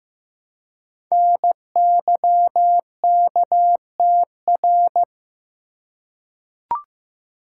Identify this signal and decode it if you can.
Morse code, heard as a keyed tone, about 15 wpm, 700 Hz. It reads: NYKTR
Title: Code Groups - 5 Letters - 15wpm